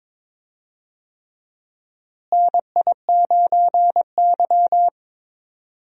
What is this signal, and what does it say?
Morse code, heard as a keyed tone, about 22 wpm, 700 Hz. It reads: NI9Y
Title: USA Call Signs - 4 Characters - Repeated 3x - 22wpm